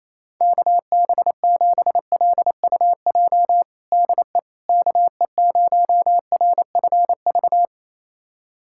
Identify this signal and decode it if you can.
Morse code, heard as a keyed tone, about 28 wpm, 700 Hz. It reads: KB7LUJ DE KE0RFV
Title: USA Callsign DE Callsign - 6 Characters - 28wpm